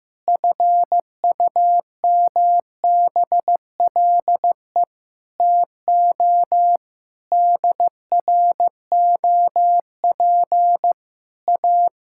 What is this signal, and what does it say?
Morse code, heard as a keyed tone, about 15 wpm, 700 Hz. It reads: FUMBLE TO DROP A
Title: Pangram Challenge v2 - 15wpm